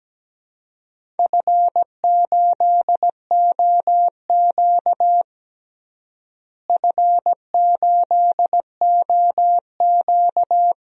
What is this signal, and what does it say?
Morse code, heard as a keyed tone, about 17 wpm, 700 Hz. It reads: F8OQ F8OQ
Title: DXCC-100 Call Signs - 4 Characters - Repeated 3x - 17wpm